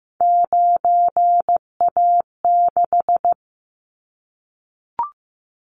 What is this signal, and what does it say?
Morse code, heard as a keyed tone, about 15 wpm, 700 Hz. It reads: 9A6
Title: Callsign Prefixes for Top 100 DXCC - 15wpm